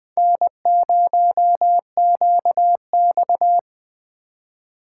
Morse code, heard as a keyed tone, about 21 wpm, 700 Hz. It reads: N0QX
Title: Speed Racing - USA 1x2 Call Signs 17wpm